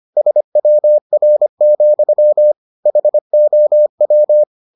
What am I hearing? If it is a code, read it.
Morse code, heard as a keyed tone, about 25 wpm, 600 Hz. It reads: SWR, HOW